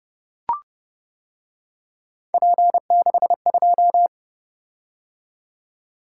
Morse code, heard as a keyed tone, about 29 wpm, 700 Hz. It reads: P62